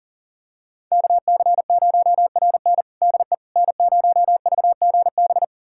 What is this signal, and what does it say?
Morse code, heard as a keyed tone, about 40 wpm, 700 Hz. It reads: KC0RN DE N0VGB
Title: USA Callsign DE Callsign 40wpm